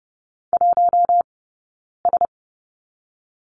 Morse code, heard as a keyed tone, about 30 wpm, 700 Hz. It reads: 1 S
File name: course / 30:10wpm / 099 Lesson 17d - Letter u - Two Characters 30wpm